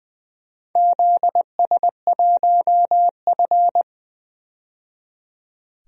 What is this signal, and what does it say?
Morse code, heard as a keyed tone, about 20 wpm, 700 Hz. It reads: ZS1F
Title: DXCC-100 Call Signs - 4 Characters - 20wpm